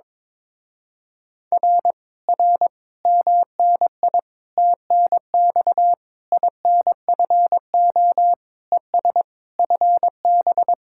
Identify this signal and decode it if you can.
Morse code, heard as a keyed tone, about 22 wpm, 700 Hz. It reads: R R MNI TNX INFO ES FB
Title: Speed-Racing Common QSO Phrases - 22wpm